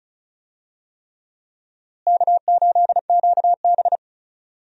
Morse code, heard as a keyed tone, about 36 wpm, 700 Hz. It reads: K8QB
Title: USA Call Signs - 4 Characters - Repeated 3x - 35wpm